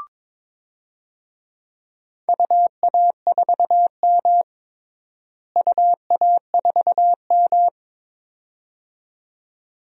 Morse code, heard as a keyed tone, about 22 wpm, 700 Hz. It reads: UA4M UA4M
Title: DXCC-100 Call Signs - 4 Characters - Repeated 2x - 22wpm